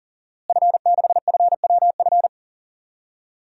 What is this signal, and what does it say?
Morse code, heard as a keyed tone, about 40 wpm, 700 Hz. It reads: F6FWF
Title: DXCC-100 Call Signs - 5 Characters - Repeated 2x - 40wpm